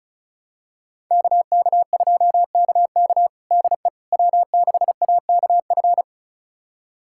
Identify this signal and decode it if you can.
Morse code, heard as a keyed tone, about 34 wpm, 700 Hz. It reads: KK2KK DE W6AKF
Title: USA Callsign DE Callsign - 5 Characters - 35wpm